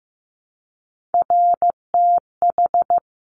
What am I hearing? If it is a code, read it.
Morse code, heard as a keyed tone, about 15 wpm, 700 Hz. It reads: RTH
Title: Top Three Letter Combinations in Words 15wpm